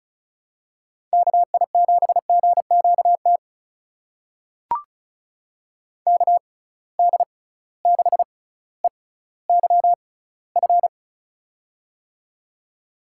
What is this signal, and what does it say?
Morse code, heard as a keyed tone, about 36 wpm, spaced as for 18 wpm, 690 Hz. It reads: KI7GQT KD6EYF
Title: USA Call Signs - 6 Characters - 6x Character Spacing - 35wpm